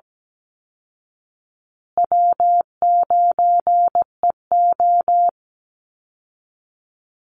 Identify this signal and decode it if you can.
Morse code, heard as a keyed tone, about 17 wpm, 700 Hz. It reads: W9EO
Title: Call Signs Repeated 2x - 17wpm